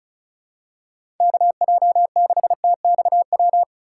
Morse code, heard as a keyed tone, about 36 wpm, 700 Hz. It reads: KJ6TXW
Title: USA Call Signs - 6 Characters - 2x Character Spacing - 35wpm